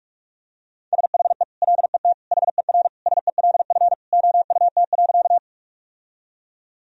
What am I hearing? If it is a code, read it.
Morse code, heard as a keyed tone, about 47 wpm, 700 Hz. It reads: SHE LET HER SELF OUT.
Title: Sentences from Top 1000 Words - Review Part 1 - 45wpm